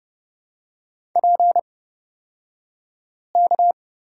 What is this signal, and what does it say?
Morse code, heard as a keyed tone, about 30 wpm, spaced as for 5 wpm, 700 Hz. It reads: PK
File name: Lesson 37d - Letter x - Two Characters 30wpm